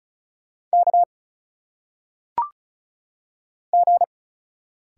Morse code, heard as a keyed tone, about 36 wpm, 700 Hz. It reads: K G
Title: Single Letters - Phonetic Pronunciation 35wpm